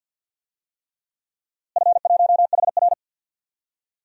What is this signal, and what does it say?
Morse code, heard as a keyed tone, about 51 wpm, 700 Hz. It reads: F1HR
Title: DXCC-100 Call Signs - 4 Characters - Repeated 3x - 50wpm